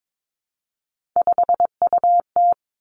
Morse code, heard as a keyed tone, about 22 wpm, 700 Hz. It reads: 5UT
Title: Three Challenging Letter-Number Combinations 22wpm